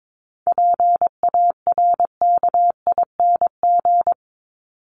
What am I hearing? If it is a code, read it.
Morse code, heard as a keyed tone, about 22 wpm, 700 Hz. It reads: PARKING
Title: Top 1501-1750 Words 22wpm